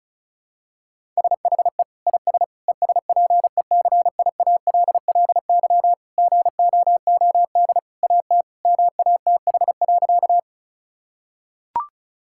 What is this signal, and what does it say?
Morse code, heard as a keyed tone, about 34 wpm, 700 Hz. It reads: SHE IS ESPECIALLY GOOD AT MATH.
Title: Sentences from Top 1000 Words - Review Part 6 - 35wpm